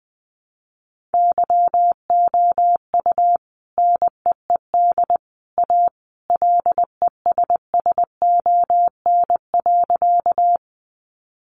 Morse code, heard as a keyed tone, about 20 wpm, 700 Hz. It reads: YOU NEED A LESSON.
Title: Sentences from Top 1200 Words 20wpm